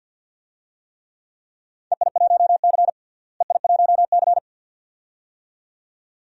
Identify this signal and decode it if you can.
Morse code, heard as a keyed tone, about 47 wpm, 700 Hz. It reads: EI1C EI1C
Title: DXCC-100 Call Signs - 4 Characters - Repeated 2x - 50wpm